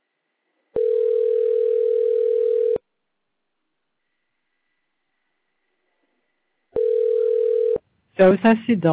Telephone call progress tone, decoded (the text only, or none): ringback tone